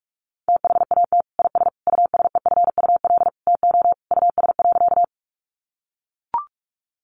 Morse code, heard as a keyed tone, about 44 wpm, 700 Hz. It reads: THAT IS USEFUL TO US.
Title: Sentences from Top 700 Words 45wpm